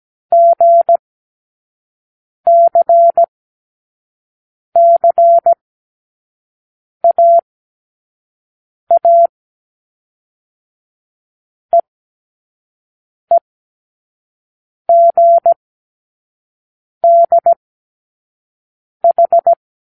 Morse code, heard as a keyed tone, about 17 wpm, spaced as for 5 wpm, 680 Hz. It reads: GCCAA EEGDH
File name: k7qo / Lesson12(A-H)